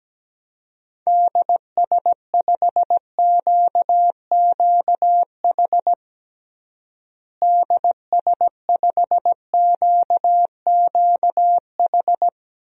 Morse code, heard as a keyed tone, about 17 wpm, 700 Hz. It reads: DS5QQH DS5QQH
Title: DXCC-100 Call Signs - 6 Characters - Repeated 3x - 17wpm